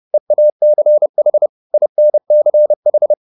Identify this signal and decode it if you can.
Morse code, heard as a keyed tone, about 30 wpm, 600 Hz. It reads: EACH INCH